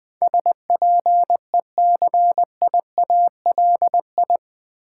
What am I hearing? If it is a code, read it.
Morse code, heard as a keyed tone, about 20 wpm, 700 Hz. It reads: SPECIALI
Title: Build A Word - 4 Letters At A Time - 11-Letter Words - 20wpm